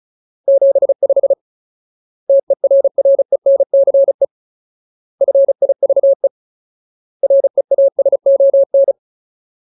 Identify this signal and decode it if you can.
Morse code, heard as a keyed tone, about 34 wpm, 550 Hz. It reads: 75 TERRENCE FIVE REASON
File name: Ultimate Word List / 35WPM Ultimate